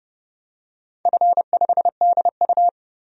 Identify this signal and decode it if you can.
Morse code, heard as a keyed tone, about 29 wpm, 700 Hz. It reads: F5DU
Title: DXCC-100 Call Signs - 4 Characters - 30wpm